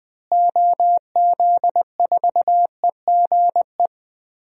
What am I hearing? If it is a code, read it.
Morse code, heard as a keyed tone, about 21 wpm, 700 Hz. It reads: OZ4EGE
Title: DXCC-100 Call Signs - 6 Characters - Repeated 3x - 20wpm